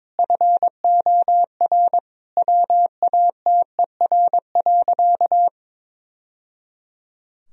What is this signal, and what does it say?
Morse code, heard as a keyed tone, about 22 wpm, 700 Hz. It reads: FOR WATER.
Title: Sentences from Top 1000 Words - Review Part 4 - 22wpm